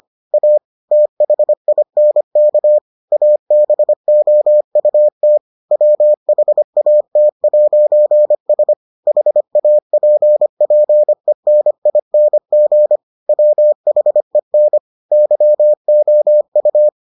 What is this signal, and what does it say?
Morse code, heard as a keyed tone, about 25 wpm, 600 Hz. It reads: A THINK ABOUT WHAT'S HAPPENING WHEN YOU